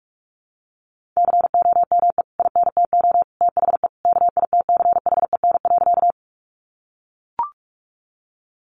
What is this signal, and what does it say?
Morse code, heard as a keyed tone, about 44 wpm, 700 Hz. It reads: COME INTO THE KITCHEN.
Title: Sentences from Top 1400 Words 45wpm